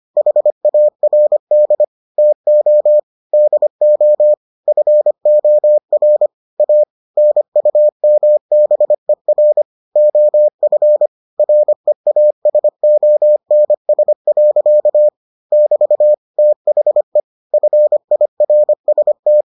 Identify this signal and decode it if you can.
Morse code, heard as a keyed tone, about 25 wpm, 600 Hz. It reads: HARD TO DO FOR A NUMBER OF REASONS. <BT> THE FIRST